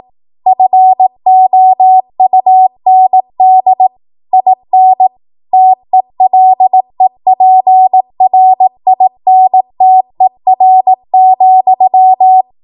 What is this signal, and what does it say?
Morse code, heard as a keyed tone, about 18 wpm, 750 Hz. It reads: FOUND IN TELEPRINTER,
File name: Morse / 18WPM